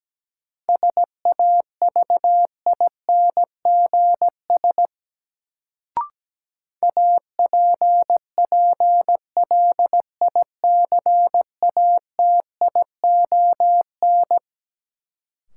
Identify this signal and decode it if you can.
Morse code, heard as a keyed tone, about 17 wpm, 700 Hz. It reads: SAVINGS APPLICATION
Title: Sentences from Top 1200 Words - New Word Drill - 17wpm